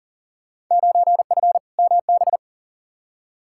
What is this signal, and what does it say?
Morse code, heard as a keyed tone, about 40 wpm, 700 Hz. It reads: 9F MB
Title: Field Day Exchanges Encore - 40wpm